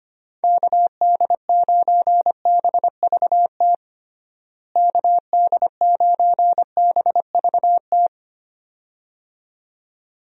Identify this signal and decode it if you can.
Morse code, heard as a keyed tone, about 25 wpm, 700 Hz. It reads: KD9BVT KD9BVT
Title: USA Call Signs - 6 Characters - Repeated 2x - 25wpm